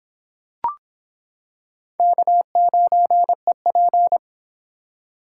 Morse code, heard as a keyed tone, about 26 wpm, 700 Hz. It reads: K9EP